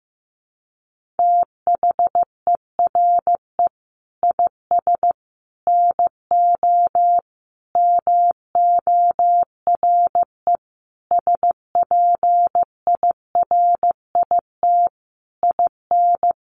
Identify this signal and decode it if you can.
Morse code, heard as a keyed tone, about 15 wpm, 700 Hz. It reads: THERE IS NO MORE SPIRIT IN
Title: Sentences from Top 1400 Words 15wpm